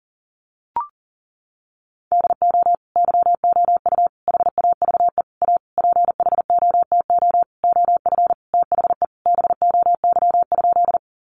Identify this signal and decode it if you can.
Morse code, heard as a keyed tone, about 40 wpm, 700 Hz. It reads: DO YOU HAVE A PHOTO OF THE BOY?
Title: Sentences from Top 1200 Words 40wpm